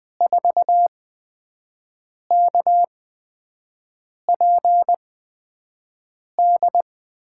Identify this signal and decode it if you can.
Morse code, heard as a keyed tone, about 20 wpm, spaced as for 5 wpm, 700 Hz. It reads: 4KPD
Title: USA Call Signs - 8x Character Spacing - 20wpm